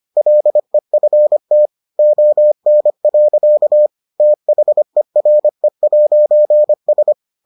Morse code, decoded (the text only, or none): LEFT ON. THERE'S